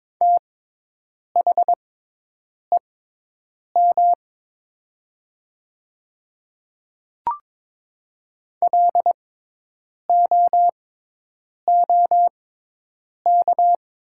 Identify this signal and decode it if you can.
Morse code, heard as a keyed tone, about 22 wpm, spaced as for 8 wpm, 700 Hz. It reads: THEM LOOK
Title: 6x Character Spacing - Top 200 Words - 22wpm